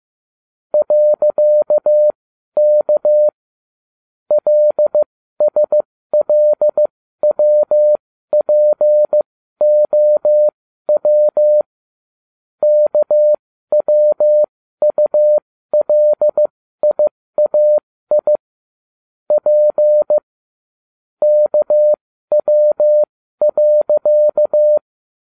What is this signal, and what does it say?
Morse code, heard as a keyed tone, about 15 wpm, spaced as for 12 wpm, 600 Hz. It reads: .K LSLWPOW KWULIAI P KW.
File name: cw / ubacw12 2